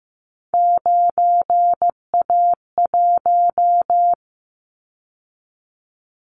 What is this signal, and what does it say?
Morse code, heard as a keyed tone, about 15 wpm, 700 Hz. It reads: 9A1
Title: Callsign Prefixes for Top 100 DXCC - 15wpm